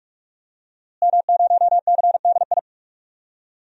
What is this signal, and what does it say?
Morse code, heard as a keyed tone, about 43 wpm, 700 Hz. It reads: M0CDI
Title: DXCC-100 Call Signs - 5 Characters - Repeated 3x - 45wpm